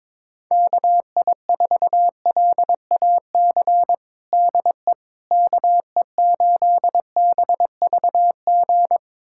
Morse code, heard as a keyed tone, about 22 wpm, 700 Hz. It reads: KI4LAC DE KE8BVG
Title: USA Callsign DE Callsign - 6 Characters - 22wpm